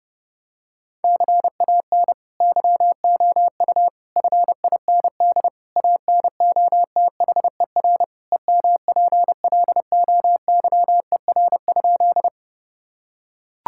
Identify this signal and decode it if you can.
Morse code, heard as a keyed tone, about 30 wpm, 700 Hz. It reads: CAN YOU FIND ANOTHER EMPLOYER?
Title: Sentences from Top 1600 Words 30wpm